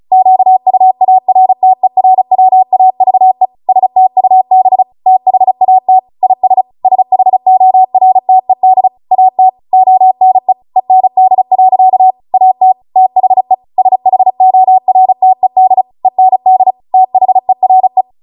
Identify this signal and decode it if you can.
Morse code, heard as a keyed tone, about 36 wpm, 750 Hz. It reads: QUARTERWAVE STUB THAT IS SHORTED AT ONE END. AT THE SHORTED END THERE